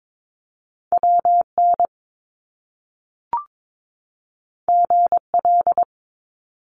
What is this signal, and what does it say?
Morse code, heard as a keyed tone, about 22 wpm, 700 Hz. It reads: WN GL
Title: Top Two Letter Combinations in Words 22wpm